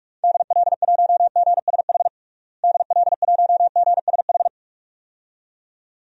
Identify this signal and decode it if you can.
Morse code, heard as a keyed tone, about 46 wpm, 700 Hz. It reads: DL1GSH DL1GSH
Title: DXCC-100 Call Signs - 6 Characters - Repeated 2x - 45wpm